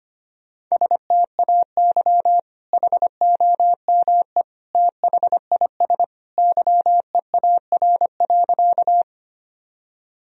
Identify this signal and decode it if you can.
Morse code, heard as a keyed tone, about 25 wpm, 700 Hz. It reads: STAY HOME THIS YEAR.
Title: Sentences from Top 1000 Words - Review Part 1 - 25wpm